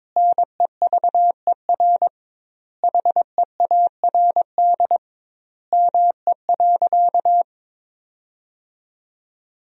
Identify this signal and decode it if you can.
Morse code, heard as a keyed tone, about 22 wpm, 700 Hz. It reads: NEVER HEARD ME.
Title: 2x Word-Spacing - Sentences from Top 400 Words - 22wpm